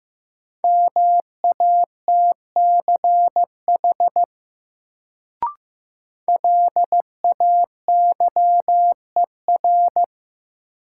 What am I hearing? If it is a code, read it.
Morse code, heard as a keyed tone, about 15 wpm, 700 Hz. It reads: MATCH LAYER
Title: Sentences from Top 1600 Words - New Word Drill - 15wpm